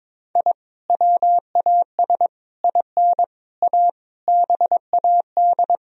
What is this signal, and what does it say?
Morse code, heard as a keyed tone, about 21 wpm, 700 Hz. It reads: I WAS IN A BAD
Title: Sentences from Top 1000 Words - Review Part 2 - 22wpm